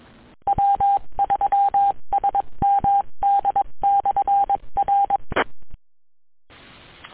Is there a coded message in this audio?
Yes: Morse code from 0.5 to 5.2 s, heard as a keyed tone, about 22 wpm, 800 Hz. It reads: W3SMD/R